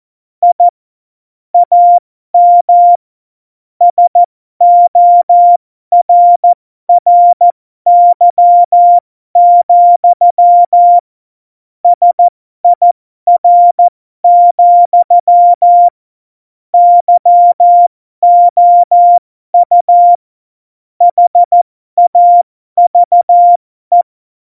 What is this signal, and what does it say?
Morse code, heard as a keyed tone, about 14 wpm, spaced as for 12 wpm, 700 Hz. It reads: I AM SORRY, SIR, YOU HAVE
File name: Morsekurs / Witze